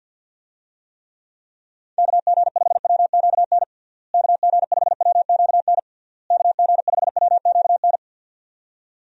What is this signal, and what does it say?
Morse code, heard as a keyed tone, about 50 wpm, 700 Hz. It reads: KG5WQN KG5WQN KG5WQN
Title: USA Call Signs - 6 Characters - Repeated 3x - 50wpm